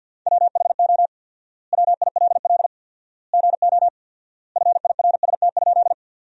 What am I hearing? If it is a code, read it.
Morse code, heard as a keyed tone, about 52 wpm, 700 Hz. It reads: WHO WILL GO FIRST?
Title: 4x Word-Spacing - Sentences from Top 100 Words - 50wpm